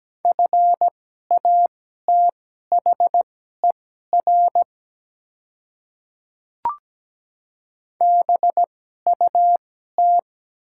Morse code, heard as a keyed tone, about 17 wpm, spaced as for 12 wpm, 700 Hz. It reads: FATHER BUT
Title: 2x Character Spacing - Top 200 Words - 17wpm